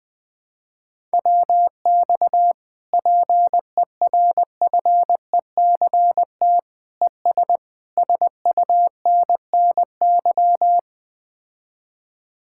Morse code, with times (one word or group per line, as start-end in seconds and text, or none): 1.1-2.5 WX
2.9-6.6 PERFECT
7.0-7.6 ES
8.0-10.8 SUNNY